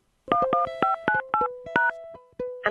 DTMF, heard as telephone touch tones, tones of about 130 ms. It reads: 11598#